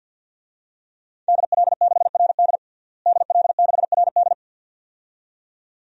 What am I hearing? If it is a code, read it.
Morse code, heard as a keyed tone, about 47 wpm, 700 Hz. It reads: DL6RD DL6RD
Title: DXCC-100 Call Signs - 5 Characters - Repeated 2x - 50wpm